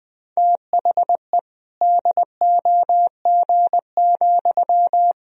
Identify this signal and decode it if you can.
Morse code, heard as a keyed tone, about 21 wpm, 700 Hz. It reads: THE DOG,
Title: Sentences from Top 1000 Words - Review Part 3 - 20wpm